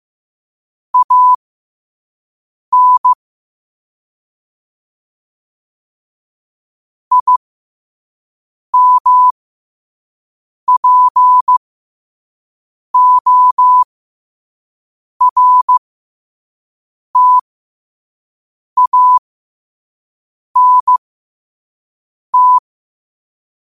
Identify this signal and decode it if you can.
Morse code, heard as a keyed tone, about 15 wpm, spaced as for 5 wpm, 1000 Hz. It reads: AN IMPORTANT